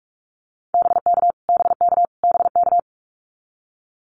Morse code, heard as a keyed tone, about 44 wpm, 700 Hz. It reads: BK BK BK
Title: Abbreviations Repeated 3x - 45wpm